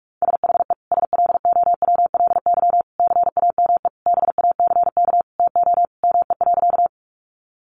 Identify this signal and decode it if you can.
Morse code, heard as a keyed tone, about 46 wpm, 700 Hz. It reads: SHE SLOWLY CAME BACK TO ME.
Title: Sentences from Top 1100 Words 45wpm